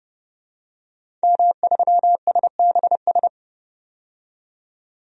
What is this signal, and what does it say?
Morse code, heard as a keyed tone, about 30 wpm, 700 Hz. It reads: M3SBS
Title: DXCC-100 Call Signs - 5 Characters - 30wpm